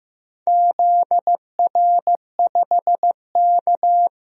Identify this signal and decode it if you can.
Morse code, heard as a keyed tone, about 15 wpm, 700 Hz. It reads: ZR5K
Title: DXCC-100 Call Signs - 4 Characters - 15wpm